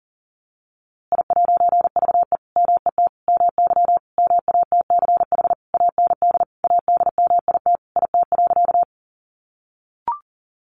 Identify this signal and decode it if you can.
Morse code, heard as a keyed tone, about 40 wpm, 700 Hz. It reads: I'VE MET MY MATCH AND ADMIT IT.
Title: A Sound Of Thunder 40wpm